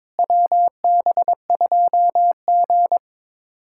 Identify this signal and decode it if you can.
Morse code, heard as a keyed tone, about 22 wpm, 700 Hz. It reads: WB2G
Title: USA Callsign DE Callsign - 4 Characters - 22wpm